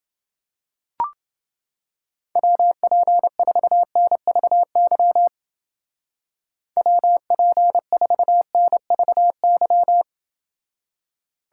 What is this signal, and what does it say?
Morse code, heard as a keyed tone, about 29 wpm, 700 Hz. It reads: WP4NVY WP4NVY